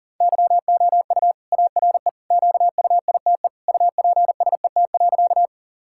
Morse code, heard as a keyed tone, about 40 wpm, 700 Hz. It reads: YOU ARE QUITE UPSET.
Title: Sentences from Top 1000 Words - Review Part 5 - 40wpm